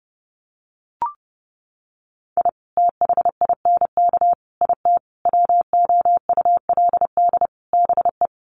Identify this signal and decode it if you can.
Morse code, heard as a keyed tone, about 30 wpm, 700 Hz. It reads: I THINK IT WOULD BE
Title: Sets of 5 Words 30wpm